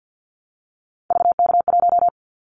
Morse code, heard as a keyed tone, about 50 wpm, 700 Hz. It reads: VK1
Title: Callsign Prefixes for Top 50 DXCC - 50wpm